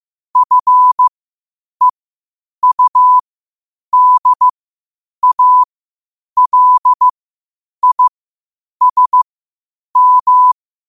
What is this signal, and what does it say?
Morse code, heard as a keyed tone, about 15 wpm, spaced as for 8 wpm, 1000 Hz. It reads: FEUDALISM
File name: qotdfiles / 8wpm